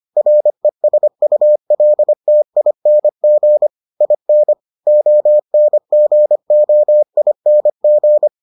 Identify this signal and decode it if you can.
Morse code, heard as a keyed tone, about 25 wpm, 600 Hz. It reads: RESULTING IN ONGOING